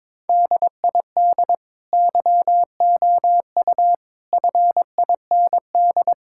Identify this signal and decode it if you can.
Morse code, heard as a keyed tone, about 22 wpm, 700 Hz. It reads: DID YOU FIND
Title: Sentences from Top 1000 Words - Review Part 6 - 22wpm